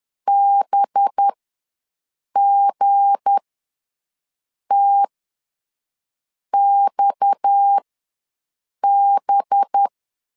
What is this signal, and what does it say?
Morse code, heard as a keyed tone, about 10 wpm, spaced as for 5 wpm, 800 Hz. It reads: BGTXB
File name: LEZIONI / 7 GF9Z / 7 G ws 10wpm 3843